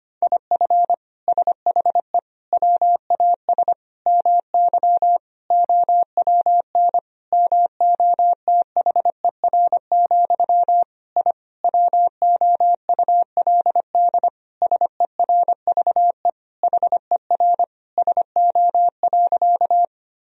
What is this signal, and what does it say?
Morse code, heard as a keyed tone, about 25 wpm, 700 Hz. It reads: IF SHE WAS MY OWN MOTHER, I WOULD SERVE HER SO.
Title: Sentences from Top 1000 Words - Review Part 2 - 25wpm